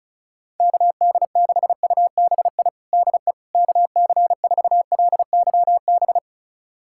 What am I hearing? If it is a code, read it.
Morse code, heard as a keyed tone, about 36 wpm, 700 Hz. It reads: KD6UBI DE KC4LYB
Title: USA Callsign DE Callsign - 6 Characters - 35wpm